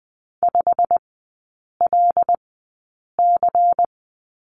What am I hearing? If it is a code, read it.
Morse code, heard as a keyed tone, about 20 wpm, spaced as for 8 wpm, 700 Hz. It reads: 5LC
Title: 093 Lesson 16e - Letter c - Three Characters 20wpm